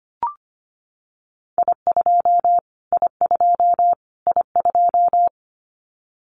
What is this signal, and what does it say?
Morse code, heard as a keyed tone, about 25 wpm, 700 Hz. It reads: I2 I2 I2
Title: Callsign Prefixes for Top 25 DXCC - Repeated 3x - 25wpm